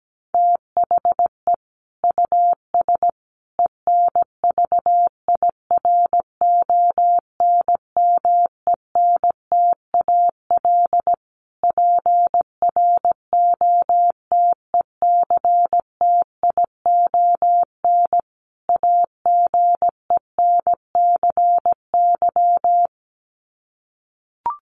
Morse code, heard as a keyed tone, about 17 wpm, 700 Hz. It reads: THE US ENVIRONMENTAL PROTECTION AGENCY
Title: Sets of 5 Words Encore2 17wpm